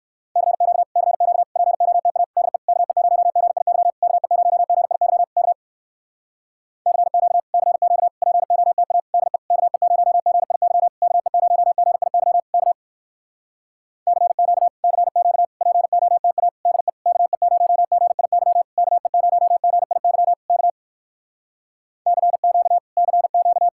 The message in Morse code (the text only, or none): CQ CQ POTA DE KE0GIY KE0GIY K CQ CQ POTA DE KE0GIY KE0GIY K CQ CQ POTA DE KE0GIY KE0GIY K CQ CQ